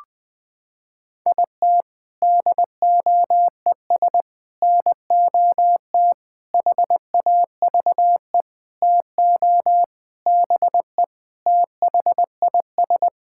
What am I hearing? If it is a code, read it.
Morse code, heard as a keyed tone, about 21 wpm, 700 Hz. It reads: IT DOES NOT HAVE TO BE THIS
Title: Sentences with 7-Word Phrases - 20wpm